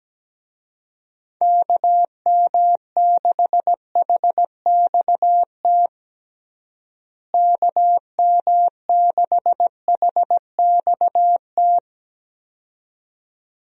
Morse code, heard as a keyed tone, about 17 wpm, 700 Hz. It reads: KM6HXT KM6HXT
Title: USA Call Signs - 6 Characters - Repeated 2x - 17wpm